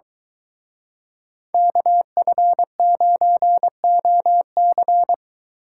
Morse code, heard as a keyed tone, about 24 wpm, 700 Hz. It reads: KF9OC